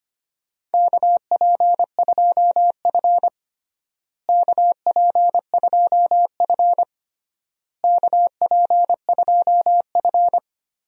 Morse code, heard as a keyed tone, about 25 wpm, 700 Hz. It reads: KP2F KP2F KP2F
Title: DXCC-100 Call Signs - 4 Characters - Repeated 3x - 25wpm